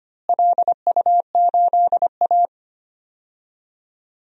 Morse code, heard as a keyed tone, about 25 wpm, 700 Hz. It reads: LU8A
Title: DXCC-100 Call Signs - 4 Characters - Repeated 3x - 25wpm